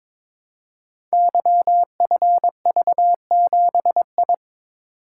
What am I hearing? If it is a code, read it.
Morse code, heard as a keyed tone, about 22 wpm, 700 Hz. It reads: YFV7I
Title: Code Groups - 5 Letter:Numbers - 22wpm